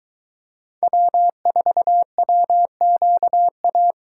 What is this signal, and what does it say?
Morse code, heard as a keyed tone, about 22 wpm, 700 Hz. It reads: W4WQA